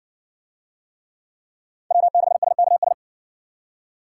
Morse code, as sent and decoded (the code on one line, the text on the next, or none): .-- -.... ... -.- ...
W6SKS